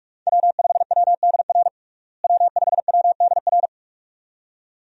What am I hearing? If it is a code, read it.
Morse code, heard as a keyed tone, about 41 wpm, 700 Hz. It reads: W5WDR W5WDR
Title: USA Call Signs - 5 Characters - Repeated 2x - 45wpm